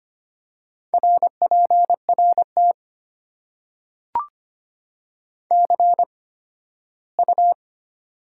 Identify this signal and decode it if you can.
Morse code, heard as a keyed tone, about 25 wpm, spaced as for 15 wpm, 700 Hz. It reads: RPRT CU
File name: Top 90 QSO Elements - 8x Character Spacing - 25wpm